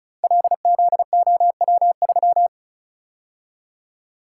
Morse code, heard as a keyed tone, about 36 wpm, 700 Hz. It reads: LZOW3
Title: Code Groups - 5 Letter:Numbers - 35wpm